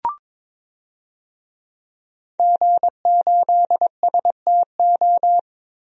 Morse code, heard as a keyed tone, about 22 wpm, 700 Hz. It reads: G8STO
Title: DXCC-100 Call Signs - 5 Characters - Repeated 3x - 22wpm